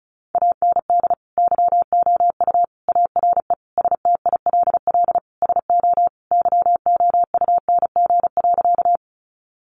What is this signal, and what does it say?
Morse code, heard as a keyed tone, about 34 wpm, 700 Hz. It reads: AND YOU ARE STILL SO YOUNG.
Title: Sentences from Top 400 Words 35wpm